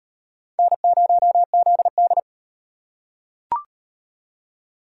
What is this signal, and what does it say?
Morse code, heard as a keyed tone, about 38 wpm, 700 Hz. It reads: N0ZD